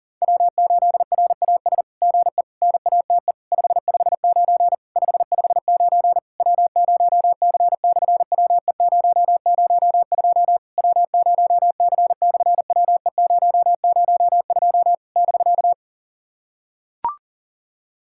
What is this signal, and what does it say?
Morse code, heard as a keyed tone, about 37 wpm, 700 Hz. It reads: W8RAS GE NATE 559 559 W0C/WE002 W0C/WE002 <BK>
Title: SOTA QSO Practice 40wpm